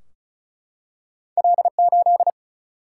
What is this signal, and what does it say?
Morse code, heard as a keyed tone, about 35 wpm, 700 Hz. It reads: L8
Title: Single-Letter Single-Number 35wpm